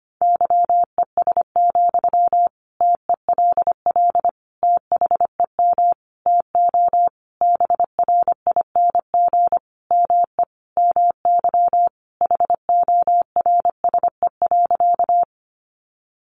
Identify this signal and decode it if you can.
Morse code, heard as a keyed tone, about 25 wpm, 700 Hz. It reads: YES, TELL THEM TO BRING ME MY HORSE.
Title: Sentences from Top 500 Words 25wpm